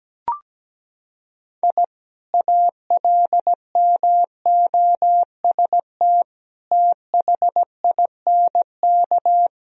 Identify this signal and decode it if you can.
Morse code, heard as a keyed tone, about 17 wpm, 700 Hz. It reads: I ALMOST THINK
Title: Sentences from Top 1000 Words - Review Part 4 - 17wpm